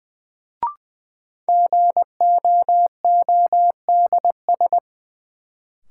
Morse code, heard as a keyed tone, about 20 wpm, 700 Hz. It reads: GOODS
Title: Sentences from Top 1000 Words - Review Part 6 - New Word Drill - 20wpm